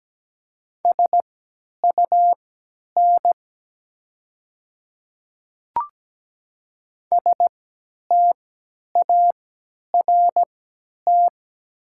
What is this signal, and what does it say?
Morse code, heard as a keyed tone, about 17 wpm, spaced as for 10 wpm, 700 Hz. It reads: SUN START
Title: 3x Character Spacing - Top 500 Words - 17wpm